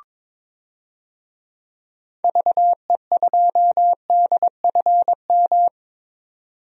Morse code, heard as a keyed tone, about 22 wpm, 700 Hz. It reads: VE2DFM
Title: DXCC-100 Call Signs - 6 Characters - 22wpm